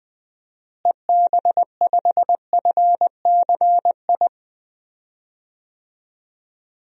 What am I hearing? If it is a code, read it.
Morse code, heard as a keyed tone, about 20 wpm, 700 Hz. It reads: EB5FCI
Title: DXCC-100 Call Signs - 6 Characters - Repeated 3x - 20wpm